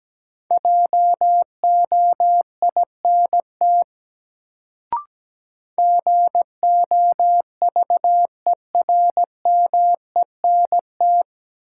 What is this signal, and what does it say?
Morse code, heard as a keyed tone, about 17 wpm, 700 Hz. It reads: JOINT GOVERMENT
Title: Sentences from Top 1300 Words - New Word Drill Alt - 17wpm